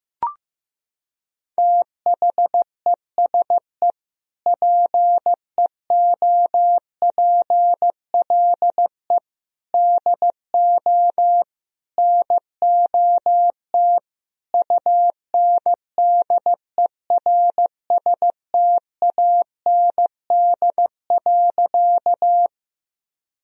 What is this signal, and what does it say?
Morse code, heard as a keyed tone, about 15 wpm, 700 Hz. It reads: THESE PEOPLE DO NOT UNDERSTAND.
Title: Sentences from Top 1000 Words - Review Part 1 - 15wpm